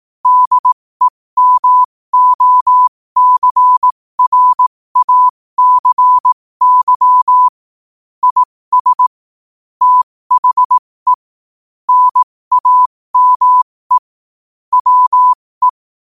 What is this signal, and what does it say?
Morse code, heard as a keyed tone, about 18 wpm, spaced as for 15 wpm, 1000 Hz. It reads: DEMOCRACY IS THE NAME WE
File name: qotdfiles / 15wpm